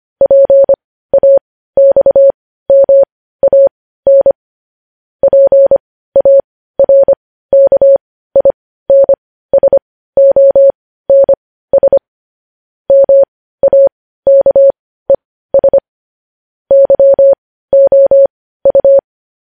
Morse code, heard as a keyed tone, about 25 wpm, spaced as for 15 wpm, 550 Hz. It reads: PAXMAN PARKINSONS MAKES YOU
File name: podcast / News headlines-1x-25x15